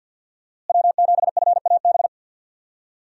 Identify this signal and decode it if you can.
Morse code, heard as a keyed tone, about 51 wpm, 700 Hz. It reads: W7FAB
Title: USA Call Signs - 5 Characters - Repeated 3x - 50wpm